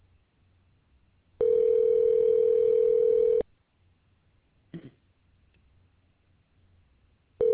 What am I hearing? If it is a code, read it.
Ringback tone (North America), 440 + 480 Hz.